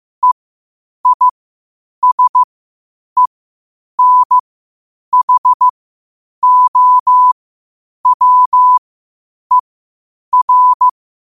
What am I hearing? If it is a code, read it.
Morse code, heard as a keyed tone, about 15 wpm, spaced as for 8 wpm, 1000 Hz. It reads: EISENHOWER
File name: qotdfiles / 8wpm